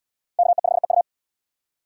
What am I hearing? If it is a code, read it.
Morse code, heard as a keyed tone, about 72 wpm, 700 Hz. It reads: KE5EU